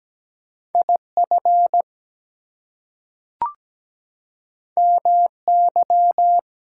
Morse code, heard as a keyed tone, about 17 wpm, 700 Hz. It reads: IF MY
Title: Two Letter Words - 17wpm